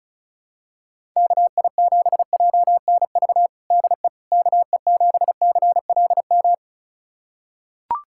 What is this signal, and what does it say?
Morse code, heard as a keyed tone, about 34 wpm, 700 Hz. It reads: KI7JNV DE KE7CLM
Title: USA Callsign DE Callsign - 6 Characters - 35wpm